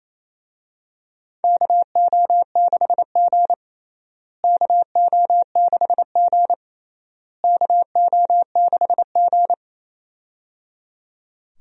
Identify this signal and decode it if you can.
Morse code, heard as a keyed tone, about 27 wpm, 700 Hz. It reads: KO6G KO6G KO6G
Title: USA Call Signs - 4 Characters - Repeated 3x - 28wpm